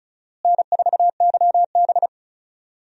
Morse code, heard as a keyed tone, about 36 wpm, 700 Hz. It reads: N4YB
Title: USA Call Signs - 4 Characters - Repeated 3x - 35wpm